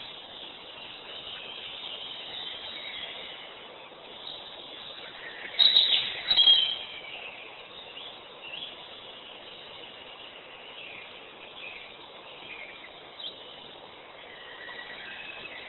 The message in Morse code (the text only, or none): none